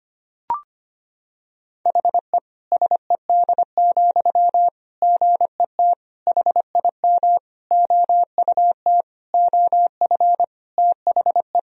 Morse code, heard as a keyed tone, about 25 wpm, 700 Hz. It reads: HE SED, GET HIM OUT OF THE
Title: Sentences from Top 400 Words and Abbreviations 25wpm